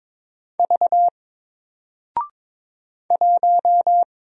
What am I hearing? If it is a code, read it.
Morse code, heard as a keyed tone, about 22 wpm, 700 Hz. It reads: V 1
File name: Single-Letter-Number-22wpm